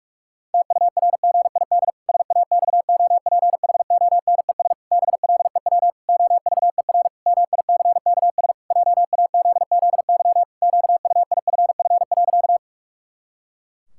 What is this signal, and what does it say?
Morse code, heard as a keyed tone, about 46 wpm, 700 Hz. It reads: TURGID SAXOPHONES BLEW OVER MICKS JAZZY QUIFF.
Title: Pangram Challenge v2 - 45wpm